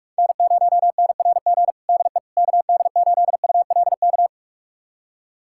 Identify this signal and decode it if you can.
Morse code, heard as a keyed tone, about 46 wpm, 700 Hz. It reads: N0NRG DE KD8ULK
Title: USA Callsign DE Callsign 45wpm